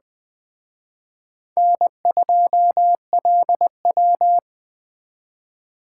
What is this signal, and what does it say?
Morse code, heard as a keyed tone, about 21 wpm, 700 Hz. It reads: N2LW
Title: Speed Racing - USA Call Signs 17wpm